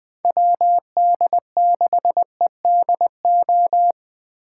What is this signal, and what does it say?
Morse code, heard as a keyed tone, about 20 wpm, 700 Hz. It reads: WD6EDO
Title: Speed Racing - USA 2x3 Call Signs 17wpm